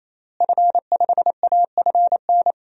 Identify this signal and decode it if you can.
Morse code, heard as a keyed tone, about 28 wpm, 700 Hz. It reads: F5AFN
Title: DXCC-100 Call Signs - 5 Characters - Repeated 3x - 28wpm